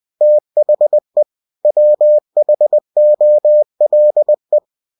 Morse code, heard as a keyed tone, about 20 wpm, 600 Hz. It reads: THE WHOLE